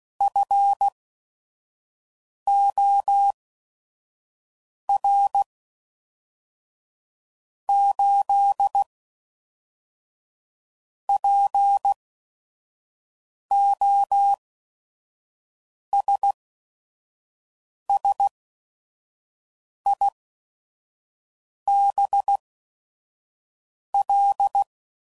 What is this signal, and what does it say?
Morse code, heard as a keyed tone, about 16 wpm, spaced as for 5 wpm, 800 Hz. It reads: FOR 8 POSSIBL
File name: CW Code Groups @5WPM / Technical Subjects / AND Gate